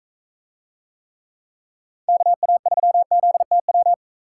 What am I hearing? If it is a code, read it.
Morse code, heard as a keyed tone, about 43 wpm, 700 Hz. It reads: KA3ZTW